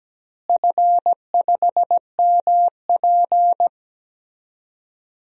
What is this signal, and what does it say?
Morse code, heard as a keyed tone, about 17 wpm, 700 Hz. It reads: F5MP
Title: DXCC-100 Call Signs - 4 Characters - Repeated 2x - 17wpm